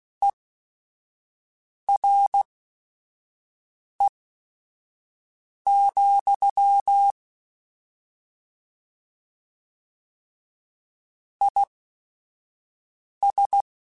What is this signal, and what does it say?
Morse code, heard as a keyed tone, about 16 wpm, spaced as for 5 wpm, 800 Hz. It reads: ERE, IS